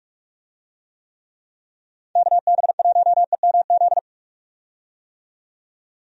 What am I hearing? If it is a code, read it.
Morse code, heard as a keyed tone, about 46 wpm, 700 Hz. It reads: KB1EMZ
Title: USA Call Signs - 6 Characters - 45wpm